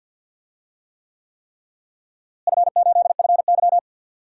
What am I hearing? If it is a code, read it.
Morse code, heard as a keyed tone, about 52 wpm, 700 Hz. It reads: F8FY
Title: DXCC-100 Call Signs - 4 Characters - Repeated 3x - 50wpm